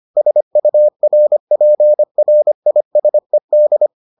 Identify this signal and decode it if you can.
Morse code, heard as a keyed tone, about 25 wpm, 600 Hz. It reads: SURPRISED